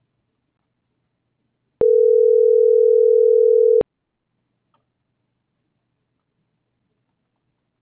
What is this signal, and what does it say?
Ringback tone (North America), 440 + 480 Hz.